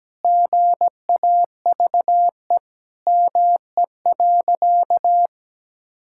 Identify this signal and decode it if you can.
Morse code, heard as a keyed tone, about 17 wpm, 700 Hz. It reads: GAVE ME.
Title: Sentences from Top 1000 Words - Review Part 6 - 17wpm